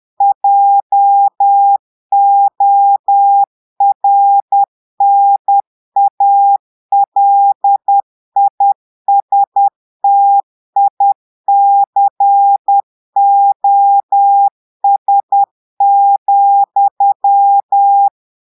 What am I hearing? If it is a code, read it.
Morse code, heard as a keyed tone, about 10 wpm, 800 Hz. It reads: JORNALISTICOS,
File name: portugais texte en clair 2 vitesse 10 mots minutes